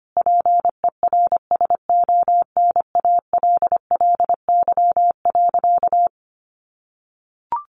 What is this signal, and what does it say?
Morse code, heard as a keyed tone, about 25 wpm, 700 Hz. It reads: PERSONALLY.
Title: Sentences from Top 1400 Words 25wpm